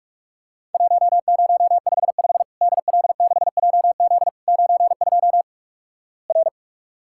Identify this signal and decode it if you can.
Morse code, heard as a keyed tone, about 46 wpm, 700 Hz. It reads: 1055 DL6JZ 92 R
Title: Worked All Europe DX Contest - QTC Exchange - 45wpm